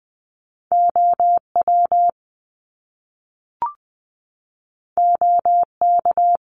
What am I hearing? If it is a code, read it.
Morse code, heard as a keyed tone, about 20 wpm, 700 Hz. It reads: OW OK
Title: Top Two Letter Combinations in Words 20wpm